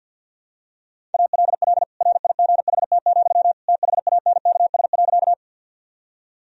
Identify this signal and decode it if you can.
Morse code, heard as a keyed tone, about 50 wpm, 700 Hz. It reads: ALL RIGHT, THANKS.
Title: Sentences from Top 1000 Words - Review Part 3 - 50wpm